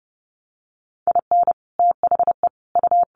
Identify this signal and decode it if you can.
Morse code, heard as a keyed tone, about 30 wpm, 700 Hz. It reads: IN THE U
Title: Sets of 5 Words Encore2 30wpm